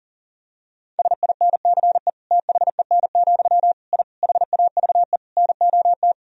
Morse code, heard as a keyed tone, about 40 wpm, 700 Hz. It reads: SINCE THEN, I HAVE NOT
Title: Sentences from Top 1000 Words - Review Part 1 - 40wpm